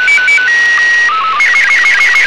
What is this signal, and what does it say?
SSTV (slow-scan television), Robot 36.
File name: ARISSAT 145mhz950 13h45 de dd1us